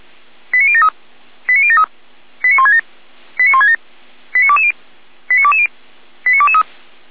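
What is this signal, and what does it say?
Five-tone selective call (ZVEI), heard as quick runs of five tones, tones of about 70 ms. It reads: 89073 89073 89167 89167 89220 89220 89223
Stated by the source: ZVEI five-tone selective call received on an SDR.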